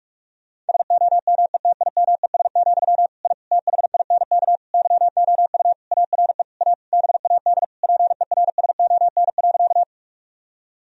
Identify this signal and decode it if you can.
Morse code, heard as a keyed tone, about 41 wpm, 700 Hz. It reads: SOMETIMES, I THINK YOU ARE A BAD PERSON.
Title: Sentences from Top 1000 Words - Review Part 3 - 45wpm